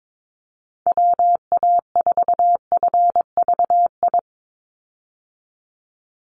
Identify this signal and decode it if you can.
Morse code, heard as a keyed tone, about 22 wpm, 700 Hz. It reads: WA4FVI
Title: USA Call Signs 22wpm